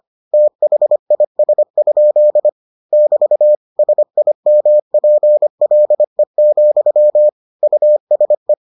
Morse code, heard as a keyed tone, about 25 wpm, 600 Hz. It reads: THIS? <BT> SIMPLE, USE